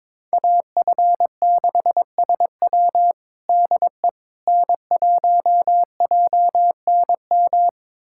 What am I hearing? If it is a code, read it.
Morse code, heard as a keyed tone, about 22 wpm, 700 Hz. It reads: AF6SW DE N1JNM
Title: USA Callsign DE Callsign - 5 Characters - 22wpm